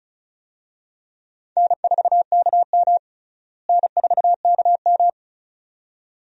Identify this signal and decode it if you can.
Morse code, heard as a keyed tone, about 36 wpm, 700 Hz. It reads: N4KM N4KM
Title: USA Call Signs - 4 Characters - Repeated 2x - 35wpm